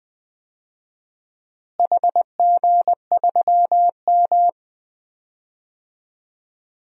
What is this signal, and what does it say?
Morse code, heard as a keyed tone, about 20 wpm, 700 Hz. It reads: HG3M
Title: DXCC-100 Call Signs - 4 Characters - Repeated 2x - 20wpm